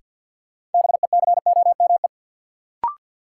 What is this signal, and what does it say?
Morse code, heard as a keyed tone, about 53 wpm, 700 Hz. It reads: BECOME
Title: Sets of 5 Words Encore - New Word Drill - 50wpm